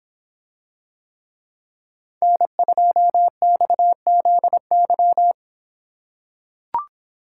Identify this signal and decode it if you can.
Morse code, heard as a keyed tone, about 26 wpm, 700 Hz. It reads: N2XZY